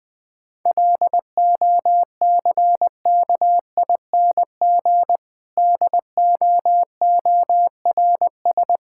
Morse code, heard as a keyed tone, about 21 wpm, 700 Hz. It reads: LOCKING DOORS
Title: Difficult Sentences with Hints - 20wpm